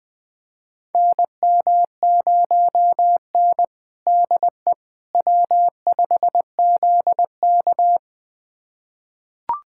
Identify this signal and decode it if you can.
Morse code, heard as a keyed tone, about 21 wpm, 700 Hz. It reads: NM0N DE W5ZK
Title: USA Callsign DE Callsign - 4 Characters - 20wpm